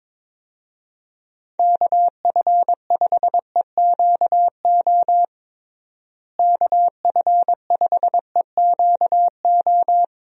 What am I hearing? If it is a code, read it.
Morse code, heard as a keyed tone, about 22 wpm, 700 Hz. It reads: KF5EQO KF5EQO
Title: USA Call Signs - 6 Characters - Repeated 2x - 22wpm